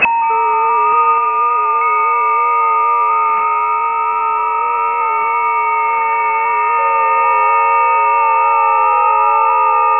FT8.